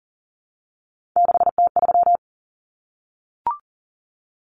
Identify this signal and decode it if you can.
Morse code, heard as a keyed tone, about 40 wpm, 700 Hz. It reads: 6T3
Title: Common Cut Numbers - 3-Digit 40wpm